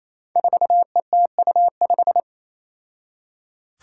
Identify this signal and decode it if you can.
Morse code, heard as a keyed tone, about 28 wpm, 700 Hz. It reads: 4ETU5
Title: Five Challenging Letter-Number Combinations 28wpm